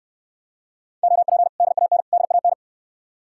Morse code, heard as a keyed tone, about 69 wpm, 700 Hz. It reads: OP DAN DAN